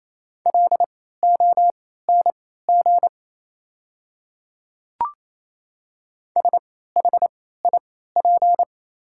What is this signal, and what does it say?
Morse code, heard as a keyed tone, about 28 wpm, spaced as for 16 wpm, 700 Hz. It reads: LONG SHIP
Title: 3x Character Spacing - Top 500 Words - 28wpm